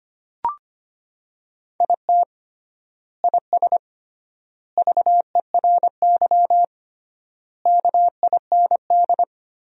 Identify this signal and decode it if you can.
Morse code, heard as a keyed tone, about 25 wpm, 700 Hz. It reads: IT IS VERY KIND
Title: 3x Word-Spacing - Sentences from Top 200 Words - 25wpm